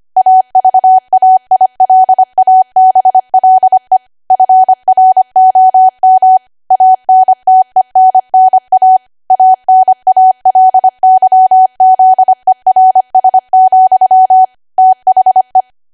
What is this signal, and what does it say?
Morse code, heard as a keyed tone, about 25 wpm, 750 Hz. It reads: AVAILABLE FROM ANTENNA ANALYZERS, THE
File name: Archive / 25 WPM / 25WPM